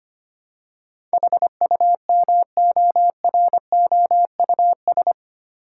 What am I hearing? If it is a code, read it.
Morse code, heard as a keyed tone, about 25 wpm, 700 Hz. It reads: HUMOROUS
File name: Eight Letter Words - 25wpm